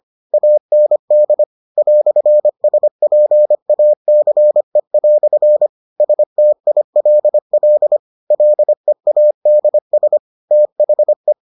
Morse code, heard as a keyed tone, about 25 wpm, 600 Hz. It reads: AND "SPACE" STILL LEADS THE